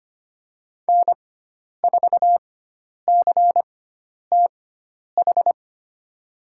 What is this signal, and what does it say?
Morse code, heard as a keyed tone, about 25 wpm, spaced as for 10 wpm, 700 Hz. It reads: N4CTH
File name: Lesson 16g - Letter c - Call Signs 25wpm